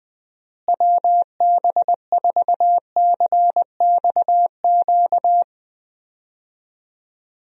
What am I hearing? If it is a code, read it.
Morse code, heard as a keyed tone, about 20 wpm, 700 Hz. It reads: WB4CXQ
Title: USA Call Signs - 6 Characters - Repeated 2x - 20wpm